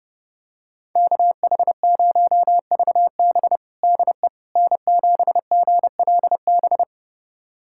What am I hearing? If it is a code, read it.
Morse code, heard as a keyed tone, about 32 wpm, 700 Hz. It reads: KH0VB DE N7GLB
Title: USA Callsign DE Callsign - 5 Characters - 30wpm